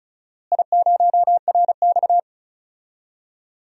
Morse code, heard as a keyed tone, about 34 wpm, 700 Hz. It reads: I0RX
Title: DXCC-100 Call Signs - 4 Characters - Repeated 2x - 35wpm